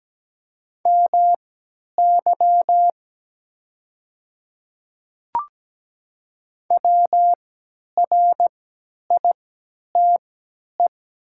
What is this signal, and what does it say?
Morse code, heard as a keyed tone, about 17 wpm, spaced as for 10 wpm, 700 Hz. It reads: MY WRITE
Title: 3x Character Spacing - Top 100 Words - 17wpm